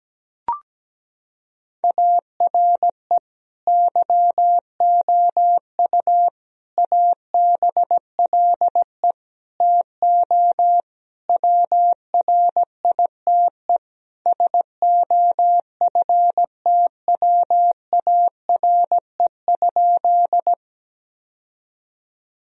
Morse code, heard as a keyed tone, about 17 wpm, 700 Hz. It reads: ARE YOU ABLE TO WRITE SOFTWARE?
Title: Sentences from Top 1000 Words - Review Part 3 - 17wpm